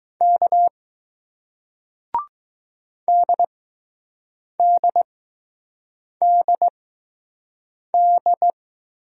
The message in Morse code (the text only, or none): K DDDD